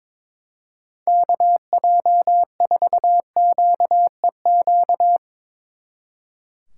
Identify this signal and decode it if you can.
Morse code, heard as a keyed tone, about 23 wpm, 700 Hz. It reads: KJ4QEQ
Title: Speed Racing - USA 2x3 Call Signs 22wpm